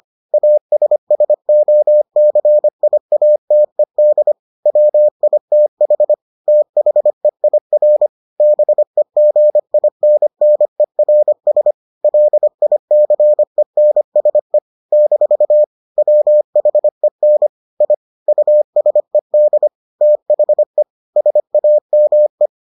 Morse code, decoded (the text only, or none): ASSOCIATED WITH THEIR BEGINNERS LICENSE - WHEN I USED THE SAME